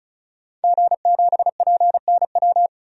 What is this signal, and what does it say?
Morse code, heard as a keyed tone, about 36 wpm, 700 Hz. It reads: G7PNW
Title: DXCC-100 Call Signs - 5 Characters - 35wpm